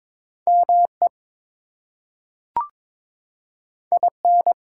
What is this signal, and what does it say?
Morse code, heard as a keyed tone, about 22 wpm, 700 Hz. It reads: ME IN
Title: Two Letter Words - 22wpm